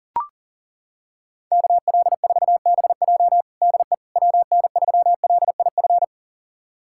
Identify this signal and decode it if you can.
Morse code, heard as a keyed tone, about 37 wpm, 700 Hz. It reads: KL4BJ DE WN3LIF
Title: USA Callsign DE Callsign 40wpm